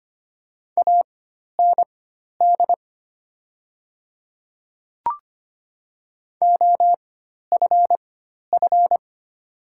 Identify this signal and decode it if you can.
Morse code, heard as a keyed tone, about 25 wpm, spaced as for 12 wpm, 700 Hz. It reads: AND OFF
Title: 4x Character Spacing - Top 200 Words - 25wpm